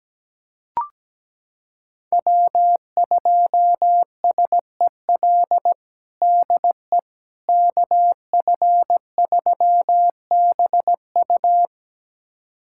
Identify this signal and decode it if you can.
Morse code, heard as a keyed tone, about 17 wpm, 700 Hz. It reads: W2SEL DE KF3BU
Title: USA Callsign DE Callsign - 5 Characters - 17wpm